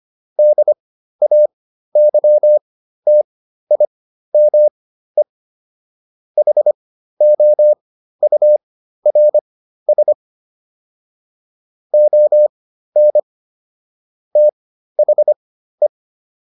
Morse code, heard as a keyed tone, about 25 wpm, spaced as for 13 wpm, 600 Hz. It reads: DAYTIME HOURS ON THE
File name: SS211 13